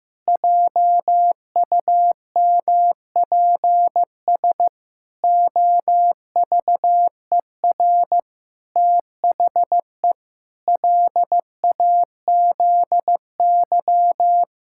Morse code, heard as a keyed tone, about 15 wpm, 700 Hz. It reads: JUMPS OVER THE LAZY
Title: Pangram Challenge v2 - 15wpm